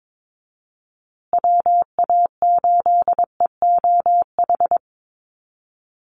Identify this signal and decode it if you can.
Morse code, heard as a keyed tone, about 22 wpm, 700 Hz. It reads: WA8EOH
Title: USA Call Signs Repeated 2x 22wpm